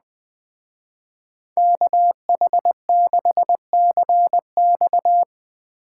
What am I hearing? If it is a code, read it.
Morse code, heard as a keyed tone, about 20 wpm, 700 Hz. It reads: KH6CX
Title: Speed Racing - USA 2x2 Call Signs 17wpm